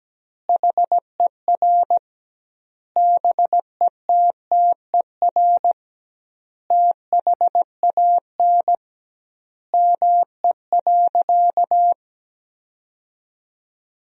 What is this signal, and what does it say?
Morse code, heard as a keyed tone, about 17 wpm, 700 Hz. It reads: HER BETTER THAN ME.
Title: 2x Word-Spacing - Sentences from Top 400 Words - 17wpm